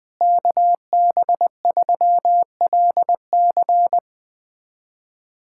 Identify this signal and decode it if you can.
Morse code, heard as a keyed tone, about 20 wpm, 700 Hz. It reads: KB3LC
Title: USA Callsign DE Callsign - 5 Characters - 20wpm